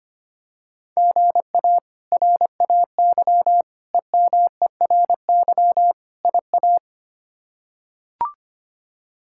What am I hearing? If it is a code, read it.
Morse code, heard as a keyed tone, about 25 wpm, 700 Hz. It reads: GA RAY EMERY IA
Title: K1USN SST QSO Practice - 25wpm